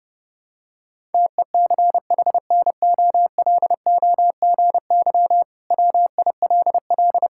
Morse code, heard as a keyed tone, about 30 wpm, 700 Hz. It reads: TECHNOLOGY WILL
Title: Sentences from Top 1000 Words - Review Part 5 - 30wpm